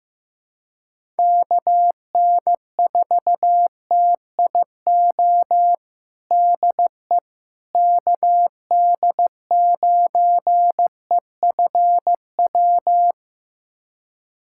Morse code, heard as a keyed tone, about 16 wpm, 700 Hz. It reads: KN4TIO DE KD9EFW
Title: USA Callsign DE Callsign - 6 Characters - 15wpm